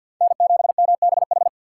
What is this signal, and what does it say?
Morse code, heard as a keyed tone, about 53 wpm, 700 Hz. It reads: N7MBH
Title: USA Call Signs - 6x Character Spacing - 50wpm